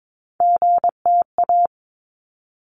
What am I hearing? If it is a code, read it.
Morse code, heard as a keyed tone, about 22 wpm, 700 Hz. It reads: GTA
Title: ARRL Sections 22wpm